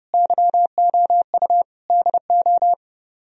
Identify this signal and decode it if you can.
Morse code, heard as a keyed tone, about 38 wpm, 700 Hz. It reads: YOU DO